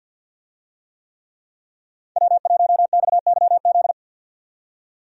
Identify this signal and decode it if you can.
Morse code, heard as a keyed tone, about 52 wpm, 700 Hz. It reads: W1XYZ
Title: USA Call Signs - 5 Characters - 50wpm